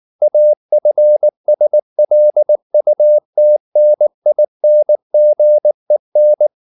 Morse code, heard as a keyed tone, about 19 wpm, 600 Hz. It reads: AFSLUTNINGEN
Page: Ferie 19 wpm